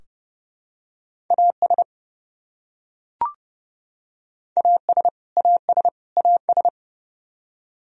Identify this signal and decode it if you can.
Morse code, heard as a keyed tone, about 29 wpm, 700 Hz. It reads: AS AS AS AS
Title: Top 100 Words Repeated 30wpm